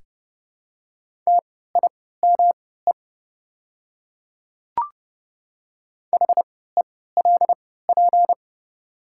Morse code, heard as a keyed tone, about 30 wpm, spaced as for 17 wpm, 700 Hz. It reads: TIME HELP
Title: 3x Character Spacing - Top 300 Words - 30wpm